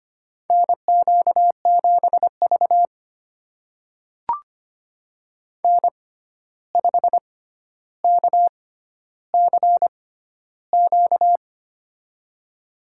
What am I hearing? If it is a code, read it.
Morse code, heard as a keyed tone, about 25 wpm, spaced as for 12 wpm, 700 Hz. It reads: NQ7V N5KCQ
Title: USA Call Signs - 6x Character Spacing - 25wpm